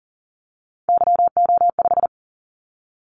Morse code, heard as a keyed tone, about 40 wpm, 700 Hz. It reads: YO5
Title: Callsign Prefixes for Top 25 DXCC - 40wpm